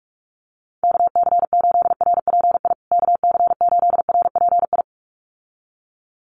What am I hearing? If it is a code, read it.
Morse code, heard as a keyed tone, about 44 wpm, 700 Hz. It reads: KC8RPI KC8RPI
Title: USA Call Signs Repeated 2x 45wpm